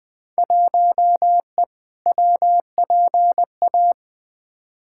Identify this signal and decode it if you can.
Morse code, heard as a keyed tone, about 20 wpm, 700 Hz. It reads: 1E WPA
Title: Field Day Exchanges - 20wpm